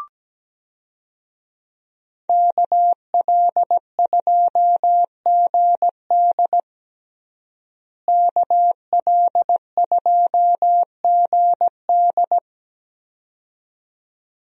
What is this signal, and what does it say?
Morse code, heard as a keyed tone, about 17 wpm, 700 Hz. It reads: KL2GD KL2GD
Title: USA Call Signs - 5 Characters - Repeated 2x - 17wpm